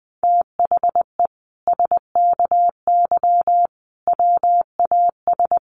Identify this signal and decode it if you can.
Morse code, heard as a keyed tone, about 20 wpm, 700 Hz. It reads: THE SKY WAS
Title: Sentences from Top 1400 Words 20wpm